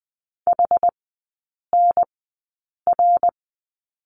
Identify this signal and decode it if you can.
Morse code, heard as a keyed tone, about 20 wpm, spaced as for 8 wpm, 700 Hz. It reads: HNR
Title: 060 Lesson 11e - Letter h - Three Characters 20wpm